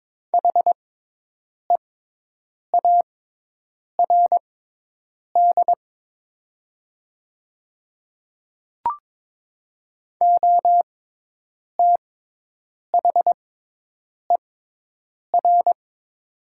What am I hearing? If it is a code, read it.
Morse code, heard as a keyed tone, about 22 wpm, spaced as for 8 wpm, 700 Hz. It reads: HEARD OTHER
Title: 6x Character Spacing - Top 500 Words - 22wpm